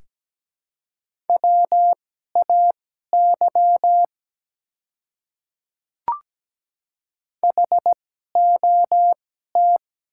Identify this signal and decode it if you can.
Morse code, heard as a keyed tone, about 17 wpm, spaced as for 12 wpm, 700 Hz. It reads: WAY HOT
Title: 2x Character Spacing - Top 100 Words - 17wpm